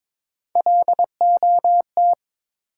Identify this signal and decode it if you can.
Morse code, heard as a keyed tone, about 22 wpm, 700 Hz. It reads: LOT
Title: Sentences from Top 500 Words - New Word Drill - 22wpm